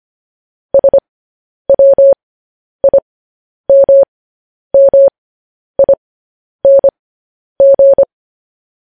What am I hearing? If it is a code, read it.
Morse code, heard as a keyed tone, about 25 wpm, spaced as for 10 wpm, 550 Hz. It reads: SWIMMING
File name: podcast / News headlines-1x-25x10